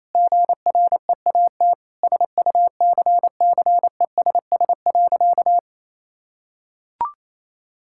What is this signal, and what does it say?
Morse code, heard as a keyed tone, about 28 wpm, 700 Hz. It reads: GREAT SUCCESS.
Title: Sentences from Top 1000 Words - Review Part 6 - 28wpm